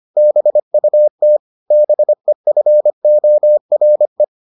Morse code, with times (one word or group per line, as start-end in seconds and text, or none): 0.2-1.4 BUT
1.7-4.2 BEFORE